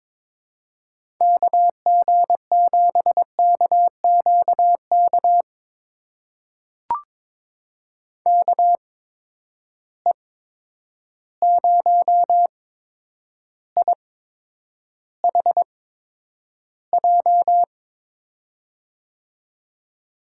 Morse code, heard as a keyed tone, about 22 wpm, spaced as for 9 wpm, 700 Hz. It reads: KG7KQK KE0IHJ
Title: USA Call Signs - 6 Characters - 8x Character Spacing - 22wpm